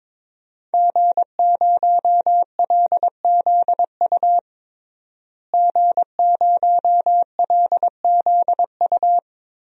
Morse code, heard as a keyed tone, about 23 wpm, 700 Hz. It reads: G0LZU G0LZU
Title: DXCC-100 Call Signs - 5 Characters - Repeated 3x - 22wpm